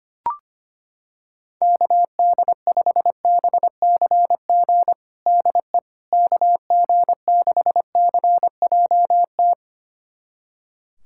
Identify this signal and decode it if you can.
Morse code, heard as a keyed tone, about 25 wpm, 700 Hz. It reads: KD5BCG DE KG6CJT
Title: USA Callsign DE Callsign 25wpm